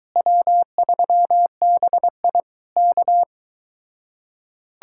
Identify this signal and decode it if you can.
Morse code, heard as a keyed tone, about 22 wpm, 700 Hz. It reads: W3BI K